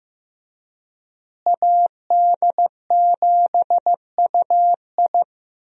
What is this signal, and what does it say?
Morse code, heard as a keyed tone, about 15 wpm, 700 Hz. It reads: AD7UI
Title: USA Call Signs - 5 Characters - Repeated 3x - 15wpm